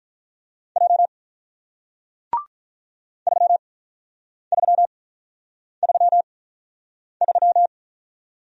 Morse code, heard as a keyed tone, about 47 wpm, spaced as for 9 wpm, 700 Hz. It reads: J 3333